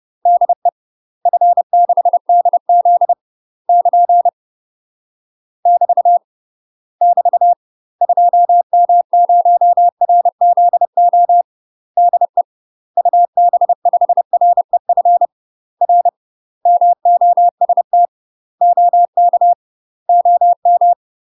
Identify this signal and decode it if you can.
Morse code, heard as a keyed tone, about 30 wpm, 700 Hz. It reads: DE F6DZ <KN> <BT> <BT> 2M0RZO DE UB5REF R MOST OK OM
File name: UK MASTER QSOs TRAINING / 30WPM / UK MASTER QSOs 0027